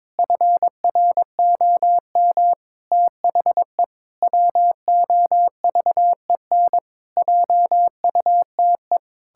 Morse code, heard as a keyed tone, about 22 wpm, 700 Hz. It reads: FROM THE WOVEN JUTE
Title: Pangram Challenge v2 - 22wpm